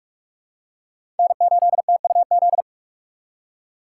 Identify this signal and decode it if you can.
Morse code, heard as a keyed tone, about 42 wpm, 700 Hz. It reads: N8TUZ
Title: USA Call Signs - 2x Character Spacing - 45wpm